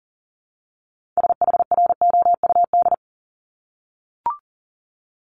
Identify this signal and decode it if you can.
Morse code, heard as a keyed tone, about 38 wpm, 700 Hz. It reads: SHROUD
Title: Multi-letter Phonemes Repeated 3x 40wpm